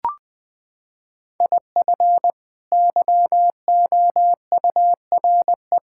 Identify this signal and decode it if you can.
Morse code, heard as a keyed tone, about 20 wpm, 700 Hz. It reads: IF YOURE
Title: Sets of 6 Words - 20wpm